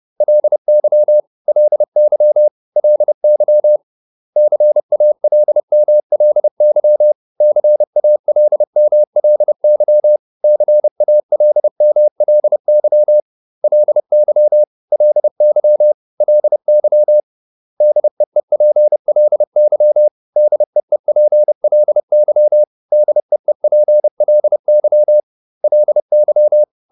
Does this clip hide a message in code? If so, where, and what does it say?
Yes: Morse code from 0.2 to 26.6 s, heard as a keyed tone, about 30 wpm, 600 Hz. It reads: LY LY LY CALMLY CALMLY CALMLY LY LY LY DEEPLY DEEPLY DEEPLY LY